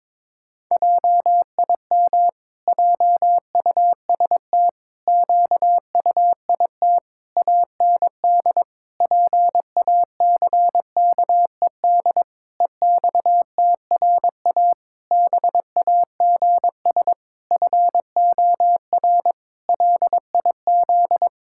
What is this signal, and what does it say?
Morse code, heard as a keyed tone, about 22 wpm, 700 Hz. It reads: JIM JUST QUIT AND PACKED EXTRA BAGS FOR LIZ
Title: Pangram Challenge v2 - 22wpm